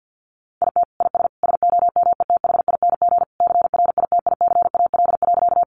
Morse code, heard as a keyed tone, about 50 wpm, 700 Hz. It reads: IT IS SOMETHING CRITICAL.
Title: Sentences from Top 1100 Words 50wpm